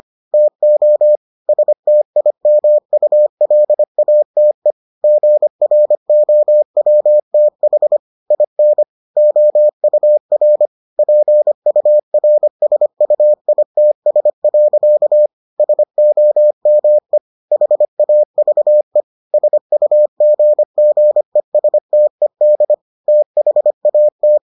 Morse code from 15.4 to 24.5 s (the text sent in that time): SOME HAVE SUGGESTED THAT